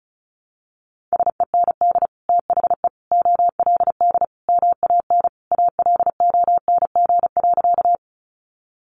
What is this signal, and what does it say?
Morse code, heard as a keyed tone, about 34 wpm, 700 Hz. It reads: SEND THE OLD MAN ALONG.
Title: Sentences from Top 1500 Words 35wpm